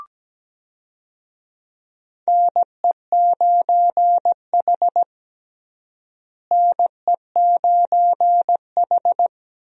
Morse code, heard as a keyed tone, about 17 wpm, 700 Hz. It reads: NE9H NE9H
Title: USA Call Signs - 4 Characters - Repeated 3x - 17wpm